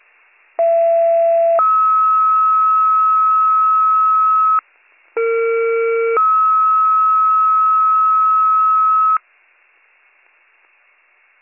Two-tone page, heard as a one-second tone, then a three-second tone, about 670 then 1290 Hz, about 460 then 1290 Hz. It is heard from 0.6 to 9.2 s.